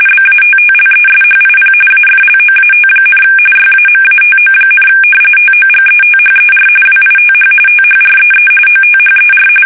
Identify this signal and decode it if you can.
RTTY, about 45.45 baud. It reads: BATTLESHIP AFLOAT, YAMATO, TOGETHER WITH THREE OTHER
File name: fsk850 / battleshipnj